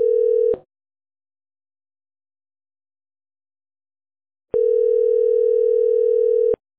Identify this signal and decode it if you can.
Ringback tone (North America), 440 + 480 Hz.